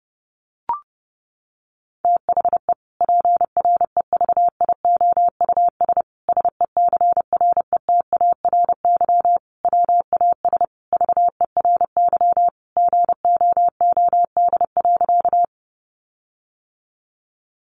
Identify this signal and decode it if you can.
Morse code, heard as a keyed tone, about 30 wpm, 700 Hz. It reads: THE PREVIOUS SECRETARY WAS VERY GOOD.
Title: Sentences from Top 1500 Words 30wpm